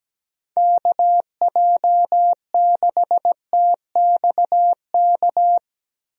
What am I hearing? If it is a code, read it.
Morse code, heard as a keyed tone, about 17 wpm, 700 Hz. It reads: KJ6TXK
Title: USA Call Signs - 6 Characters - 17wpm